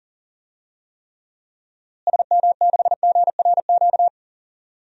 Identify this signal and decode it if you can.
Morse code, heard as a keyed tone, about 40 wpm, 700 Hz. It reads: SM6GRQ
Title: DXCC-100 Call Signs - 6 Characters - Repeated 2x - 40wpm